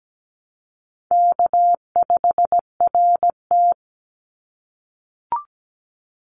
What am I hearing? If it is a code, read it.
Morse code, heard as a keyed tone, about 17 wpm, 700 Hz. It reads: K5RT
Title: Call Signs - 17wpm